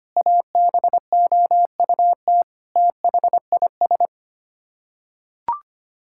Morse code, heard as a keyed tone, about 25 wpm, 700 Hz. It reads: ABOUT THIS
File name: Sets of 9 Words - 25wpm